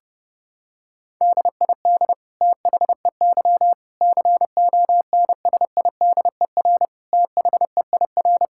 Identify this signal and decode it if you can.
Morse code, heard as a keyed tone, about 30 wpm, 700 Hz. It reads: DID THEY CONSIDER THEIR
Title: Sentences from Top 1000 Words - Review Part 6 - 30wpm